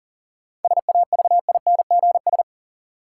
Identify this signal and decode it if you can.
Morse code, heard as a keyed tone, about 40 wpm, 700 Hz. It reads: SAVINGS
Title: Sentences from Top 1200 Words - New Word Drill Alt - 40wpm